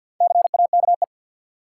Morse code, heard as a keyed tone, about 52 wpm, 700 Hz. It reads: CAKE
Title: Four Letter Words - 50wpm